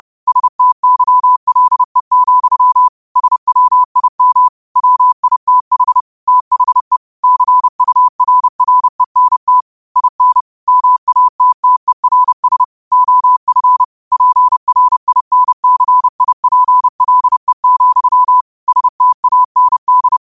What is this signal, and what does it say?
Morse code, heard as a keyed tone, about 30 wpm, 1000 Hz. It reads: STYLE, SWIM WITH THE CURRENT IN MATTERS OF PRINCIPLE, STAND
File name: qotdfiles / 30wpm